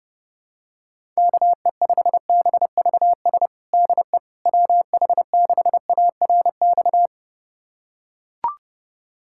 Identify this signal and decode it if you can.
Morse code, heard as a keyed tone, about 32 wpm, 700 Hz. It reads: KE5BVS DE WH6ARX
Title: USA Callsign DE Callsign - 6 Characters - 30wpm